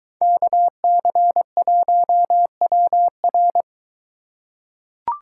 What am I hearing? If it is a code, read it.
Morse code, heard as a keyed tone, about 21 wpm, 700 Hz. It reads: KC1WR